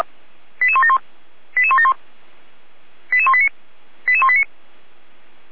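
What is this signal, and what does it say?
Five-tone selective call (ZVEI), heard as quick runs of five tones, tones of about 70 ms. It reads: 88171 88171 88189 88189
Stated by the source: ZVEI five-tone selective call 88163 received by SDR.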